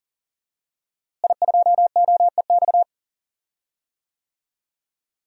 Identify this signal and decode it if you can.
Morse code, heard as a keyed tone, about 40 wpm, 700 Hz. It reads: I2OEX
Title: DXCC-100 Call Signs - 5 Characters - Repeated 2x - 40wpm